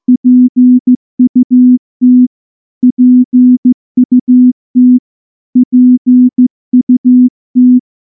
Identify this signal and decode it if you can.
Morse code, heard as a keyed tone, about 14 wpm, 250 Hz. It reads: PUT PUT PUT
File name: morse-training-15WPM